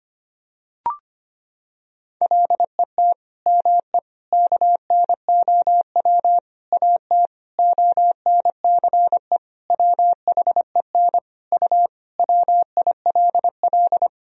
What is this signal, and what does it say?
Morse code, heard as a keyed tone, about 25 wpm, 700 Hz. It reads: LET ME KNOW AT ONCE WHEN U WILL
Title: Sentences from Top 300 Words and Abbreviations 25wpm